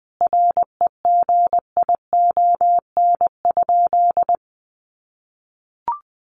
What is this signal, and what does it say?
Morse code, heard as a keyed tone, about 20 wpm, 700 Hz. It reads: REGION?
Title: Sentences from Top 1500 Words 20wpm